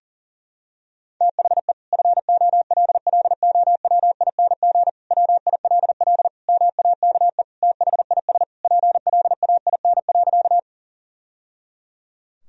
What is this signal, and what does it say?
Morse code, heard as a keyed tone, about 40 wpm, 700 Hz. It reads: THE FOLLOWING WILL MAKE THIS PLAIN.
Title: Sentences from Top 1000 Words - Review Part 4 - 40wpm